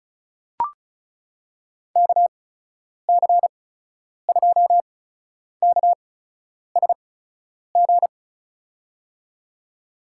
Morse code, heard as a keyed tone, about 36 wpm, spaced as for 10 wpm, 700 Hz. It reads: KC2KSG
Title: USA Call Signs - 6 Characters - 8x Character Spacing - 35wpm